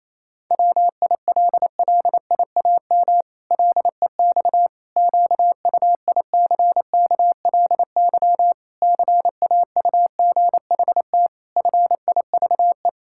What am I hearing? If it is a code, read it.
Morse code, heard as a keyed tone, about 28 wpm, 700 Hz. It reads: WILLIAM LEX QUICKLY CAUGHT FIVE
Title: Pangram Challenge v2 - 28wpm